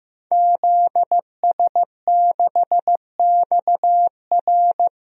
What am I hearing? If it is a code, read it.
Morse code, heard as a keyed tone, about 15 wpm, 700 Hz. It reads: ZS6XR
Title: DXCC-100 Call Signs - 5 Characters - Repeated 3x - 15wpm